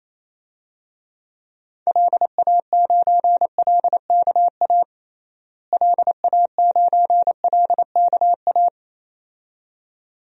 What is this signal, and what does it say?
Morse code, heard as a keyed tone, about 28 wpm, 700 Hz. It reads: LA9LKA LA9LKA
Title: DXCC-100 Call Signs - 6 Characters - Repeated 2x - 28wpm